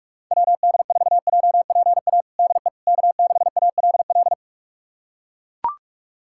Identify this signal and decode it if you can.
Morse code, heard as a keyed tone, about 43 wpm, 700 Hz. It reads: WD4JPA DE K6ALL
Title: USA Callsign DE Callsign 45wpm